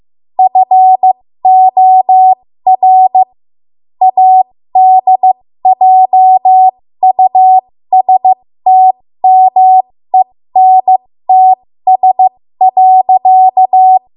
Morse code, heard as a keyed tone, about 15 wpm, spaced as for 13 wpm, 750 Hz. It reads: FOR ADJUSTMENTS.
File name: Morse / 13WPM